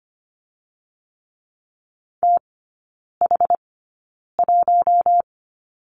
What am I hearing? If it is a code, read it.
Morse code, heard as a keyed tone, about 25 wpm, spaced as for 9 wpm, 690 Hz. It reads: TH1
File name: Lesson 14e - Number 2 - Three Characters 25wpm